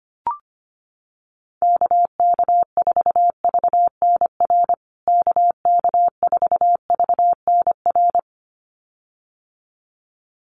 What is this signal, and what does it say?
Morse code, heard as a keyed tone, about 25 wpm, 700 Hz. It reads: KK4VNR KK4VNR
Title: USA Call Signs Repeated 2x 25wpm